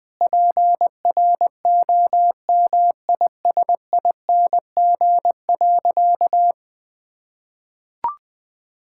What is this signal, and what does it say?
Morse code, heard as a keyed tone, about 20 wpm, 700 Hz. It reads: PROMISING.
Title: Sentences with 5-Word Phrases - 20wpm